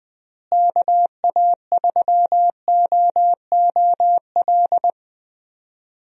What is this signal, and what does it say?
Morse code, heard as a keyed tone, about 20 wpm, 700 Hz. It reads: KA3OOL
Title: USA Call Signs - 6 Characters - Repeated 3x - 20wpm